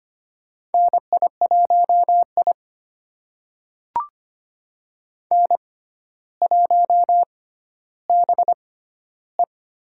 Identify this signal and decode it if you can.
Morse code, heard as a keyed tone, about 25 wpm, spaced as for 13 wpm, 700 Hz. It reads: NI1I N1BE
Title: USA Call Signs - 4 Characters - 6x Character Spacing - 25wpm